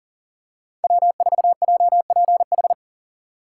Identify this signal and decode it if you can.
Morse code, heard as a keyed tone, about 37 wpm, 700 Hz. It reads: W4JPH
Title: K1USN SST QSO Practice - 40wpm